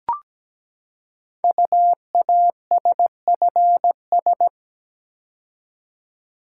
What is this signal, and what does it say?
Morse code, heard as a keyed tone, about 17 wpm, 700 Hz. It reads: UASFS
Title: Code Groups - 5 Letters - 17wpm